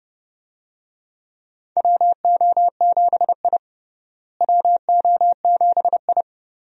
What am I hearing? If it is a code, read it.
Morse code, heard as a keyed tone, about 30 wpm, 700 Hz. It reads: WO7I WO7I
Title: USA Call Signs - 4 Characters - Repeated 3x - 30wpm